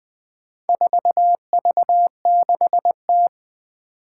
Morse code, heard as a keyed tone, about 20 wpm, 700 Hz. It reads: 4V6T
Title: Four Challenging Letter-Number Combinations 20wpm